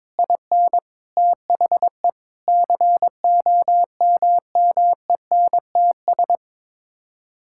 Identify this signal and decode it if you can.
Morse code, heard as a keyed tone, about 22 wpm, 700 Hz. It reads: IN THE COMMENTS
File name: practice / 22wpm / Sets of 6 Words - 22wpm